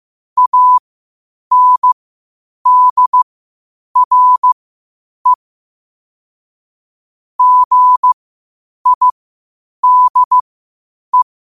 Morse code, heard as a keyed tone, about 15 wpm, spaced as for 8 wpm, 1000 Hz. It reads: ANDRE GIDE
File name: qotdfiles / 8wpm